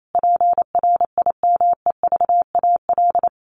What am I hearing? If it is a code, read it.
Morse code, heard as a keyed tone, about 28 wpm, 700 Hz. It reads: PRIMEVAL
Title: A Sound Of Thunder 28wpm